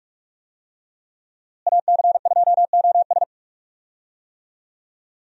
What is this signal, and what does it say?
Morse code, heard as a keyed tone, about 47 wpm, 700 Hz. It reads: AC2OS
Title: USA Call Signs - 5 Characters - Repeated 2x - 45wpm